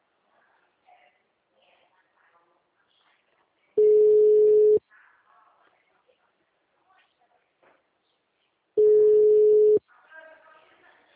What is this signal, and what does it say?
Ringback tone (Europe, 425 Hz), with speech.
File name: out- -2042- - - .8467